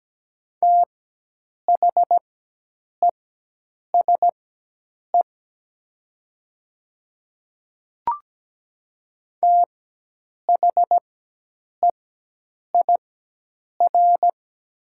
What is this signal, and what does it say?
Morse code, heard as a keyed tone, about 17 wpm, spaced as for 8 wpm, 700 Hz. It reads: THESE THEIR
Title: 4x Character Spacing - Top 500 Words - 17wpm